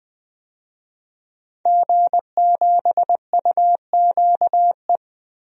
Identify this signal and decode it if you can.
Morse code, heard as a keyed tone, about 20 wpm, 700 Hz. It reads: G7UQE
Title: DXCC-100 Call Signs - 5 Characters - Repeated 2x - 20wpm